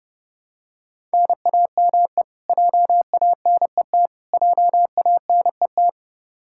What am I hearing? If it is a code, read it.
Morse code, heard as a keyed tone, about 30 wpm, 700 Hz. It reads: NAME JANET JANET
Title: Practice QSO - Exchange 1 - Beginning - 30wpm